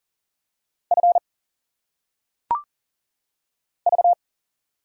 Morse code, heard as a keyed tone, about 43 wpm, 700 Hz. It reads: F V
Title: Single Letters - Phonetic Pronunciation 40wpm